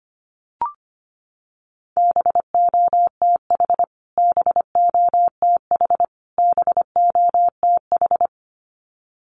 Morse code, heard as a keyed tone, about 25 wpm, 700 Hz. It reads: BOTH BOTH BOTH
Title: Top 500 Words Repeated 25wpm